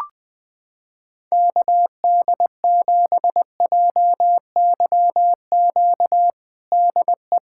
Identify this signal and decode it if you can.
Morse code, heard as a keyed tone, about 20 wpm, 700 Hz. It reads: KD7JYQ DE
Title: USA Callsign DE Callsign 20wpm